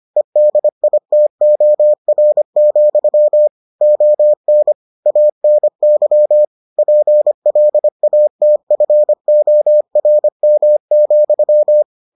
Morse code, heard as a keyed tone, about 25 wpm, 600 Hz. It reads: EDITOR, ON ANY PLATFORM,